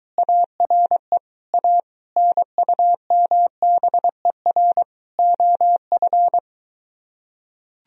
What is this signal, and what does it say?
Morse code, heard as a keyed tone, about 23 wpm, 700 Hz. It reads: ARE A NUMBER OF